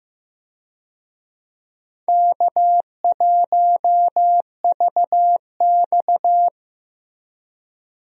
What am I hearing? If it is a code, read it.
Morse code, heard as a keyed tone, about 15 wpm, 700 Hz. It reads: K1VX
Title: USA Call Signs - 4 Characters - Repeated 3x - 15wpm